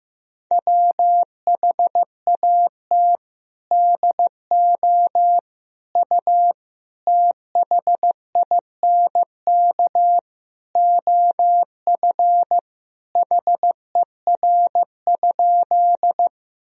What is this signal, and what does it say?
Morse code, heard as a keyed tone, about 16 wpm, 700 Hz. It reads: WHAT DO U THINK OF HER?
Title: Sentences from Top 200 Words and Abbreviations 15wpm